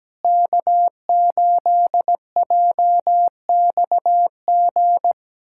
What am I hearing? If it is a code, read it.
Morse code, heard as a keyed tone, about 17 wpm, 700 Hz. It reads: K8JXG
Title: USA Callsign DE Callsign 17wpm